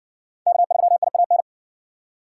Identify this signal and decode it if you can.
Morse code, heard as a keyed tone, about 54 wpm, 700 Hz. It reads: K3IAN